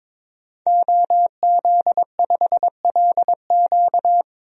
Morse code, heard as a keyed tone, about 22 wpm, 700 Hz. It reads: OZ5LQ
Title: DXCC-100 Call Signs - 5 Characters - 22wpm